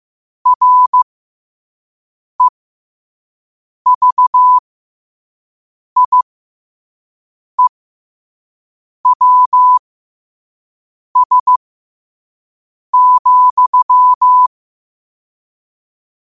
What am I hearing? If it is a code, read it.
Morse code, heard as a keyed tone, about 15 wpm, spaced as for 5 wpm, 1000 Hz. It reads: REVIEWS,
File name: qotdfiles / 5wpm